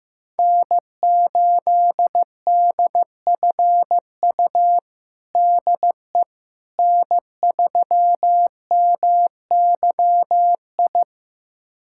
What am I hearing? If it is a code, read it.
Morse code, heard as a keyed tone, about 15 wpm, 700 Hz. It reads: N8DFU DE N3MYI
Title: USA Callsign DE Callsign - 5 Characters - 15wpm